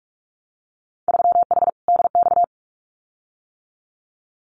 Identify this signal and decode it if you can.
Morse code, heard as a keyed tone, about 46 wpm, 690 Hz. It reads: 3H DX
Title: Winter Field Day Exchanges Encore 45wpm